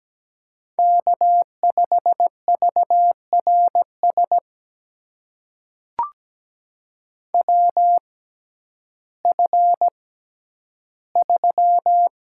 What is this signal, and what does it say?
Morse code, heard as a keyed tone, about 17 wpm, spaced as for 10 wpm, 700 Hz. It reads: K5VRS WF3
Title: USA Call Signs - 6x Character Spacing - 17wpm